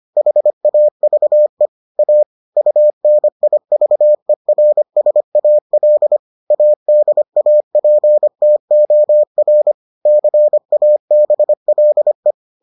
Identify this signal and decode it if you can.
Morse code, heard as a keyed tone, about 25 wpm, 600 Hz. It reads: HAVE A UNIVERSAL ADAPTOR CABLE